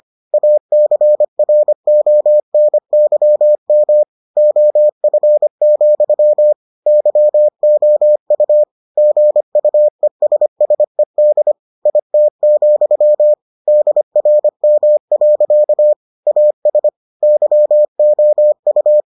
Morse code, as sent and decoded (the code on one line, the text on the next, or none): .- -.-. .-. --- -. -.-- -- / --- ..-. --..-- / -.-- --- ..- / --. ..- . ... ... -.. / .. - --..-- / -.. .-. -- .-.-.- / .- ... / -.-- --- ..-
ACRONYM OF, YOU GUESSED IT, DRM. AS YOU